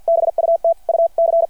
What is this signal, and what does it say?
Morse code, heard as a keyed tone, about 53 wpm, 690 Hz. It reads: BUT U C
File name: archived / CRSnet